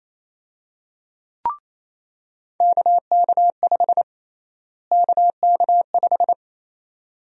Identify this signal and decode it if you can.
Morse code, heard as a keyed tone, about 28 wpm, 700 Hz. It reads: KK5 KK5
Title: USA Call Sign Prefixes - Repeated 2x - 28wpm